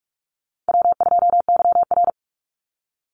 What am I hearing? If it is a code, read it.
Morse code, heard as a keyed tone, about 44 wpm, 700 Hz. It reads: W2YR
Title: Easy Call Signs Repeated 3x - 45wpm